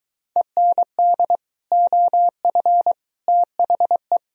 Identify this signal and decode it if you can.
Morse code, heard as a keyed tone, about 23 wpm, 700 Hz. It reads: END OF THE